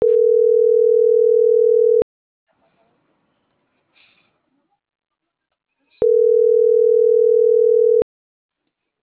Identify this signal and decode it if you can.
Ringback tone (North America), 440 + 480 Hz.